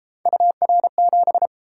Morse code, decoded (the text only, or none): UR7